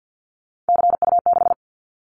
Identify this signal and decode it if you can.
Morse code, heard as a keyed tone, about 50 wpm, 700 Hz. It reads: CU6
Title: Callsign Prefixes for Top 50 DXCC - 50wpm